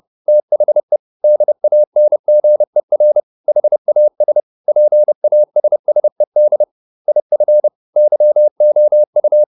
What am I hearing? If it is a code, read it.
Morse code, heard as a keyed tone, about 30 wpm, 600 Hz. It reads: THE DANGER HAS PASSED IF YOU